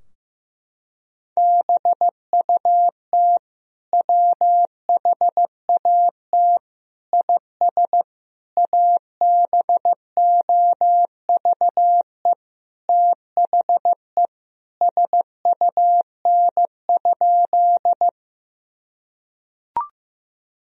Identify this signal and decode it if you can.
Morse code, heard as a keyed tone, about 15 wpm, 700 Hz. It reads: BUT WHAT IS ABOVE THE SUN?
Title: Sentences from Top 1000 Words - Review Part 2 - 15wpm